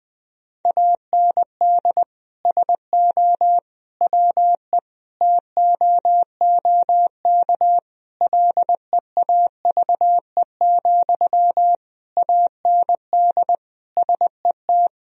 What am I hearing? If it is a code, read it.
Morse code, heard as a keyed tone, about 20 wpm, 700 Hz. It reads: AND SO WE TOOK LEAVE, AND SET
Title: Sentences from Top 1000 Words - Review Part 1 - 20wpm